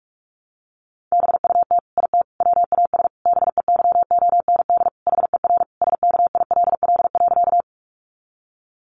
Morse code, heard as a keyed tone, about 44 wpm, 700 Hz. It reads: BUT IT WAS BEYOND HER SKILL.
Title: Sentences from Top 1200 Words 45wpm